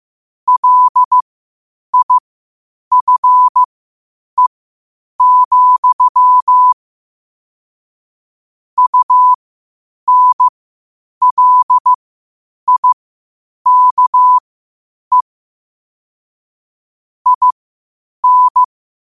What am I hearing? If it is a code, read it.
Morse code, heard as a keyed tone, about 15 wpm, spaced as for 8 wpm, 1000 Hz. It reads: LIFE, UNLIKE IN
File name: qotdfiles / 8wpm